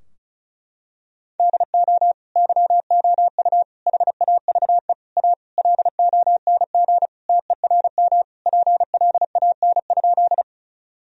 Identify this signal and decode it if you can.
Morse code, heard as a keyed tone, about 36 wpm, 700 Hz. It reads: DO YOU HAVE A LONG TERM PLAN?
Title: Sentences from Top 1000 Words - Review Part 4 - 35wpm